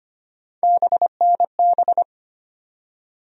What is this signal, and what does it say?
Morse code, heard as a keyed tone, about 25 wpm, 700 Hz. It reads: BNB
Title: USA Call Sign Postfixes - 3 Letters 25wpm